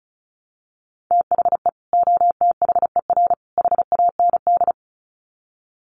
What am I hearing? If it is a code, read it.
Morse code, heard as a keyed tone, about 34 wpm, 700 Hz. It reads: THE OTHER HAND
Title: Sets of 3 Words 35wpm